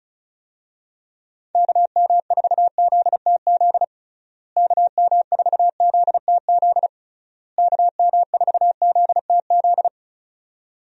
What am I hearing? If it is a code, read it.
Morse code, heard as a keyed tone, about 36 wpm, 700 Hz. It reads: KM4ZTZ KM4ZTZ KM4ZTZ
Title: USA Call Signs - 6 Characters - Repeated 3x - 35wpm